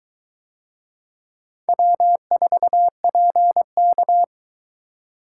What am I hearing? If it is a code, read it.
Morse code, heard as a keyed tone, about 20 wpm, 700 Hz. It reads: W4PK